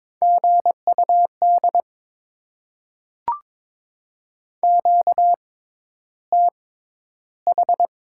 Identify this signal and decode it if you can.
Morse code, heard as a keyed tone, about 22 wpm, spaced as for 11 wpm, 700 Hz. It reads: GUD QTH
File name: Top 20 QSO Elements - 6x Character Spacing - 22wpm